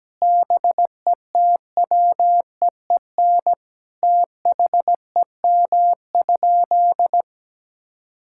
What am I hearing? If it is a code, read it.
Morse code, heard as a keyed tone, about 17 wpm, 700 Hz. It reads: BETWEEN THEM?
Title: Sentences from Top 1000 Words - Review Part 6 - 17wpm